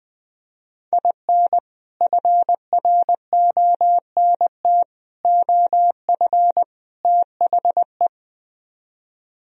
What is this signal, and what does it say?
Morse code, heard as a keyed tone, about 20 wpm, 700 Hz. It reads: IN FRONT OF THE
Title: Speed-Racing - Sets of 4 Words 20wpm